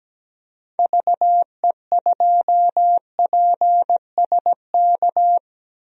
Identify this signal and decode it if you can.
Morse code, heard as a keyed tone, about 17 wpm, 700 Hz. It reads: VE2PSK
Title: DXCC-100 Call Signs - 6 Characters - 17wpm